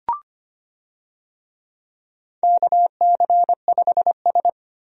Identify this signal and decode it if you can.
Morse code, heard as a keyed tone, about 25 wpm, 700 Hz. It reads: KC5S
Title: USA Call Signs - 4 Characters - Repeated 2x - 25wpm